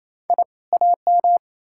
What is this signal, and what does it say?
Morse code, heard as a keyed tone, about 38 wpm, spaced as for 33 wpm, 700 Hz. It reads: I AM
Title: Speed-Racing Common QSO Phrases - 28wpm